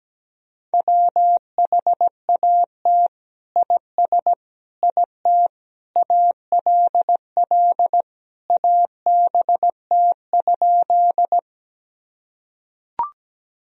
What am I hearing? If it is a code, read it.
Morse code, heard as a keyed tone, about 17 wpm, 700 Hz. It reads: WHAT IS IT ALL ABT?
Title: Sentences from Top 100 Words and Abbreviations 17wpm